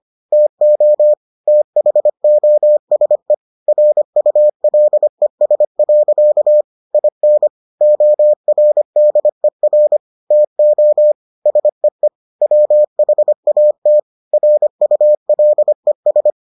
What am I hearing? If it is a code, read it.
Morse code, heard as a keyed tone, about 25 wpm, 600 Hz. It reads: TO THOSE RULES. IN ORDER TO SEE WHAT RULES